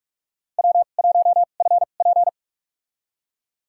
Morse code, heard as a keyed tone, about 46 wpm, spaced as for 32 wpm, 700 Hz. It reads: W1FP
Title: USA Call Signs - 4 Characters - 2x Character Spacing - 45wpm